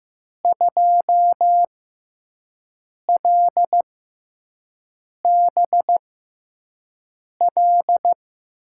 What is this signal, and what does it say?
Morse code, heard as a keyed tone, about 15 wpm, spaced as for 5 wpm, 700 Hz. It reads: 2LBL
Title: USA Call Signs - 6x Character Spacing - 15wpm